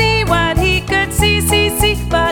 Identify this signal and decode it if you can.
There is no code.